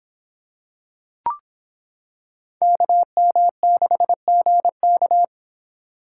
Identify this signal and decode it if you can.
Morse code, heard as a keyed tone, about 25 wpm, 700 Hz. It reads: KM6GK